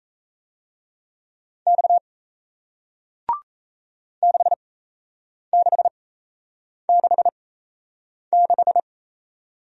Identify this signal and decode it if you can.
Morse code, heard as a keyed tone, about 38 wpm, spaced as for 8 wpm, 700 Hz. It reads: X 6666